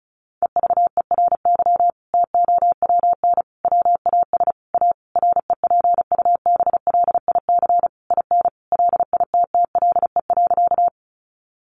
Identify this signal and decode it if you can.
Morse code, heard as a keyed tone, about 34 wpm, 700 Hz. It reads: EVERY TOWN WAS A REPUBLIC IN LITTLE.
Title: Sentences from Top 1500 Words 35wpm